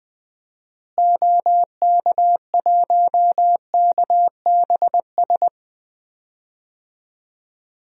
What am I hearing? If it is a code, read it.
Morse code, heard as a keyed tone, about 20 wpm, 700 Hz. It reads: OK1KBS
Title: DXCC-100 Call Signs - 6 Characters - Repeated 2x - 20wpm